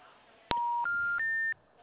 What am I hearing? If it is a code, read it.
Special information tone.